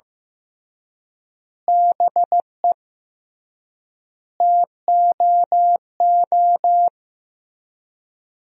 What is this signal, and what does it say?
Morse code, heard as a keyed tone, about 16 wpm, 700 Hz. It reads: BE TOO
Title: 3x Word-Spacing - Sentences from Top 300 Words - 15wpm